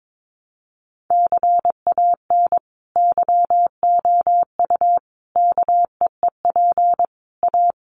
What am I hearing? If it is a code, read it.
Morse code, heard as a keyed tone, about 22 wpm, 700 Hz. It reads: CAN YOU KEEP A
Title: Sentences from Top 1200 Words 22wpm